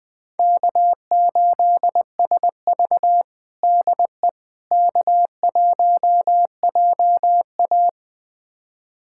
Morse code, heard as a keyed tone, about 21 wpm, 700 Hz. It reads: K8SV DE K1JA
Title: USA Callsign DE Callsign - 4 Characters - 20wpm